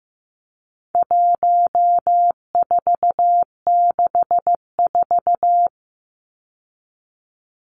Morse code, heard as a keyed tone, about 15 wpm, 700 Hz. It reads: 1464
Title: Numbers - 4-Digit 15wpm